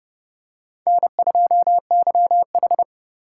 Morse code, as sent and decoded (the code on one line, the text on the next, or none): -. ..--- -.-- ....
N2YH